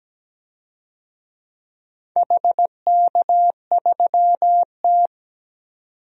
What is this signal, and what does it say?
Morse code, heard as a keyed tone, about 17 wpm, 700 Hz. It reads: HK3T
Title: DXCC-100 Call Signs - 4 Characters - 17wpm